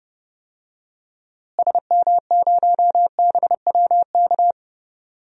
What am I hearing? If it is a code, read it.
Morse code, heard as a keyed tone, about 32 wpm, 700 Hz. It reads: SM0BWK
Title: DXCC-100 Call Signs - 6 Characters - Repeated 3x - 30wpm